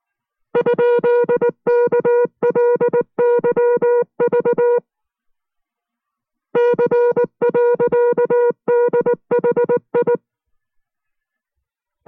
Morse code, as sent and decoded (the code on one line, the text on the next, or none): ..--.. -.- .-.. -.-- ...- / -.-. .-.-.- -.. .... ..
?KLYV C.DHI